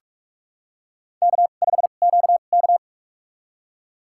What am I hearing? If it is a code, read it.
Morse code, heard as a keyed tone, about 47 wpm, spaced as for 33 wpm, 700 Hz. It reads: K5QK
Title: USA Call Signs - 2x Character Spacing - 45wpm